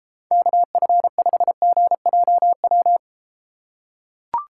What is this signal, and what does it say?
Morse code, heard as a keyed tone, about 33 wpm, 700 Hz. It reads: KF5GJW